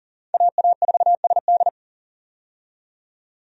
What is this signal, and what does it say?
Morse code, heard as a keyed tone, about 40 wpm, 700 Hz. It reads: AA4SD
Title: USA Call Signs - 5 Characters - 40wpm